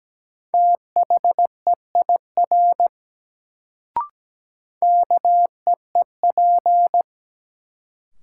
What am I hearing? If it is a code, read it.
Morse code, heard as a keyed tone, about 17 wpm, 700 Hz. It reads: THEIR KEEP
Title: Sentences from Top 300 Words - New Word Drill - 17wpm